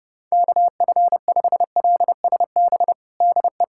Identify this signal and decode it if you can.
Morse code, heard as a keyed tone, about 30 wpm, 700 Hz. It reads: KF5LSB DE
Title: USA Callsign DE Callsign 30wpm